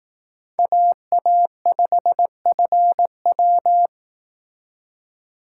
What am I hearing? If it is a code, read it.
Morse code, heard as a keyed tone, about 18 wpm, 700 Hz. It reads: AA5FW
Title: Speed Racing - USA 2x2 Call Signs 15wpm